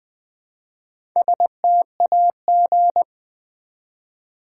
Morse code, heard as a keyed tone, about 20 wpm, 700 Hz. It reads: STAG
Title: Build A Word - 1 Letter At A Time - 5-Letter Words - 20wpm